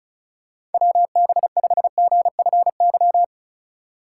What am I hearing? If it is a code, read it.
Morse code, heard as a keyed tone, about 36 wpm, 700 Hz. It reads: WB5GFY
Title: USA Call Signs - 3x Character Spacing - 35wpm